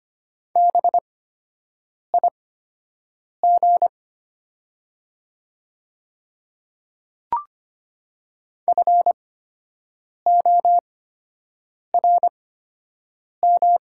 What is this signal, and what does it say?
Morse code, heard as a keyed tone, about 25 wpm, spaced as for 7 wpm, 700 Hz. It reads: BIG FORM
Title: 8x Character Spacing - Top 200 Words - 25wpm